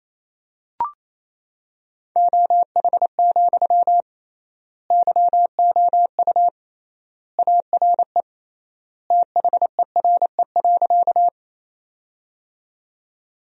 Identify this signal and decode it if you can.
Morse code, heard as a keyed tone, about 28 wpm, 700 Hz. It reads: OH, YOU ARE THERE.
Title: 3x Word-Spacing - Sentences from Top 500 Words - 28wpm